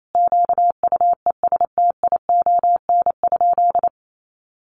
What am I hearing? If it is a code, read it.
Morse code, heard as a keyed tone, about 28 wpm, 700 Hz. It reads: QUESTION?
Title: Sentences from Top 400 Words 28wpm